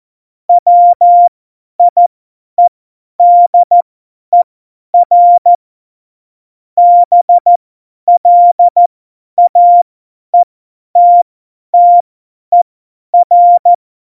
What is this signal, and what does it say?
Morse code, heard as a keyed tone, about 14 wpm, spaced as for 10 wpm, 700 Hz. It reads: WIEDER BLAETTER